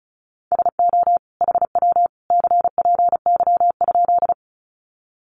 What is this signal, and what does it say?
Morse code, heard as a keyed tone, about 34 wpm, 700 Hz. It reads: SO HW CPY?
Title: Common QSO Phrases 35wpm